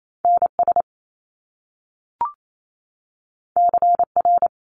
Morse code, heard as a keyed tone, about 28 wpm, 700 Hz. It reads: NS CR
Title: Top Two Letter Combinations in Words 28wpm